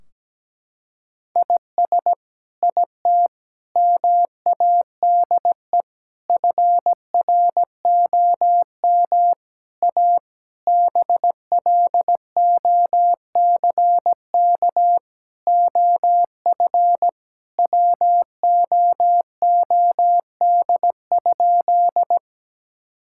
Morse code, heard as a keyed tone, about 17 wpm, 700 Hz. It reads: IS IT MADE FROM A BLOCK OF WOOD?
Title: Sentences from Top 1600 Words 17wpm